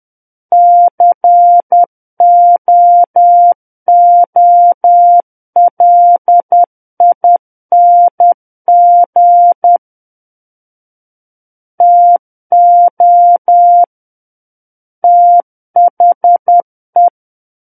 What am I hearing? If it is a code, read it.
Morse code, heard as a keyed tone, about 10 wpm, 700 Hz. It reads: COOLING TO THE
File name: Book One / 01-Chapter 1 - 10 WPM